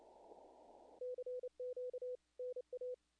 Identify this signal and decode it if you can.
Morse code, heard as a keyed tone, about 30 wpm, 500 Hz. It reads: CQ NA
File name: contests / naqp-cw-k5tr-op-k5pi- -1700